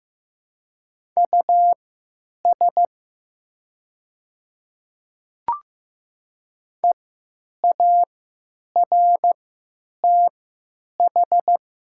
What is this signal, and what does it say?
Morse code, heard as a keyed tone, about 15 wpm, spaced as for 8 wpm, 700 Hz. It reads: US EARTH
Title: 3x Character Spacing - Top 200 Words - 15wpm